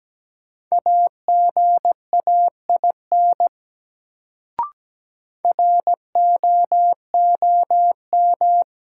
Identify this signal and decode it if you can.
Morse code, heard as a keyed tone, about 17 wpm, 700 Hz. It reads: AGAIN ROOM
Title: Sets of 4 Words - New Word Drill - 17wpm